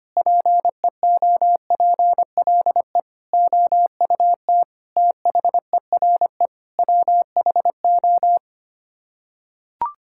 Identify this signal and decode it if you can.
Morse code, heard as a keyed tone, about 25 wpm, 700 Hz. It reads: PEOPLE OUT THERE WHO
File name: Sets of 9 Words - 25wpm